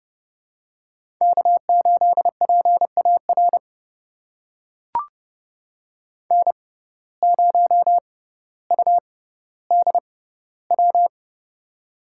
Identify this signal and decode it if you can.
Morse code, heard as a keyed tone, about 30 wpm, spaced as for 15 wpm, 700 Hz. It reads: K8PAR N0UDW
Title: USA Call Signs - 5 Characters - 6x Character Spacing - 30wpm